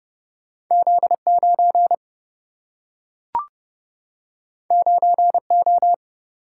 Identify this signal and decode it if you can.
Morse code, heard as a keyed tone, about 30 wpm, 700 Hz. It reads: Z9 9O
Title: Single-Letter Single-Number 30wpm